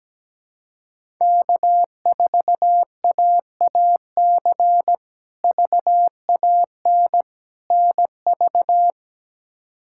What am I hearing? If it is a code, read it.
Morse code, heard as a keyed tone, about 17 wpm, 700 Hz. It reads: K4AAC VAN NV
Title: K1USN SST QSO Practice - 17wpm